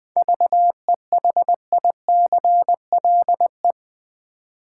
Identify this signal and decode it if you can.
Morse code, heard as a keyed tone, about 20 wpm, 700 Hz. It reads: VEHICLE
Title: Seven Letter Words - 20wpm